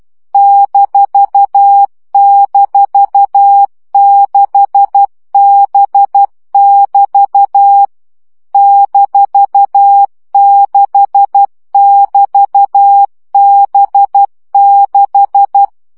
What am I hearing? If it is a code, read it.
Morse code, heard as a keyed tone, about 12 wpm, 800 Hz. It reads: --6B<BT> -6<BT>B6
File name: CW gyakorlas b 6 egyenlo minusz 10 baud